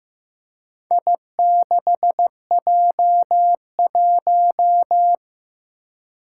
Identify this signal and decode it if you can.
Morse code, heard as a keyed tone, about 15 wpm, 700 Hz. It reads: I6J1
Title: Four Challenging Letter-Number Combinations 15wpm